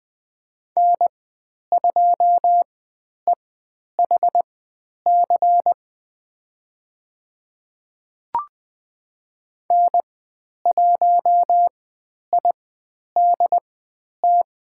Morse code, heard as a keyed tone, about 20 wpm, spaced as for 10 wpm, 700 Hz. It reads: N2EHC N1IDT
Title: 102 Lesson 17g - Letter u - Call Signs 20wpm